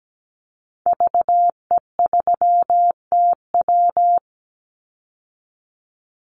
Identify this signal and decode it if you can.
Morse code, heard as a keyed tone, about 17 wpm, 700 Hz. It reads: VE3TW
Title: Call Signs Repeated 3x - 17wpm